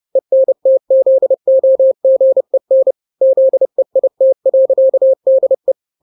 Morse code, heard as a keyed tone, about 30 wpm, 530 Hz. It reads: ENTZOGEN ZEIT.DE
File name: 29wpm